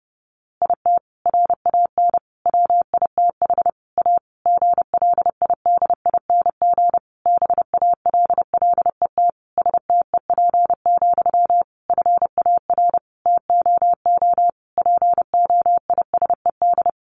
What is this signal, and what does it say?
Morse code, heard as a keyed tone, about 30 wpm, 700 Hz. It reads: IT RAN WITH A GLIDING BALLET STEP, FAR TOO POISED
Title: A Sound Of Thunder 30wpm